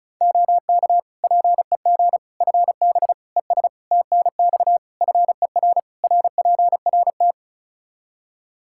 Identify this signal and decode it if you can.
Morse code, heard as a keyed tone, about 34 wpm, 700 Hz. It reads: OK PEG FB ES TNX FER RPRT
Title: Practice QSO - Exchange 2 - Info - 35wpm